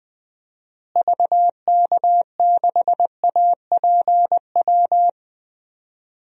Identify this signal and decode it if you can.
Morse code, heard as a keyed tone, about 20 wpm, 700 Hz. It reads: VK6APW
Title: DXCC-100 Call Signs - 6 Characters - Repeated 3x - 20wpm